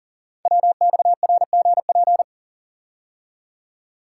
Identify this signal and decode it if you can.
Morse code, heard as a keyed tone, about 38 wpm, 690 Hz. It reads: WXRGP
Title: Code Groups - 5 Letters - 40wpm